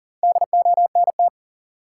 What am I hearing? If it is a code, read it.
Morse code, heard as a keyed tone, about 38 wpm, 700 Hz. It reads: DONT
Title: Top 180 QSO Elements 40wpm